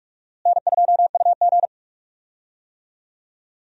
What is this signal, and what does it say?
Morse code, heard as a keyed tone, about 46 wpm, 700 Hz. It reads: N2UG
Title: USA Call Signs - 4 Characters - Repeated 2x - 45wpm